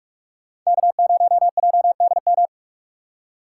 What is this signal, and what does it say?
Morse code, heard as a keyed tone, about 58 wpm, 700 Hz. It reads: K0JDM